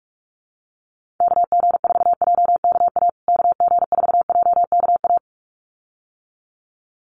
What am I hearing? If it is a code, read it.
Morse code, heard as a keyed tone, about 44 wpm, 700 Hz. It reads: KG4JKA KG4JKA
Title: USA Call Signs Repeated 2x 45wpm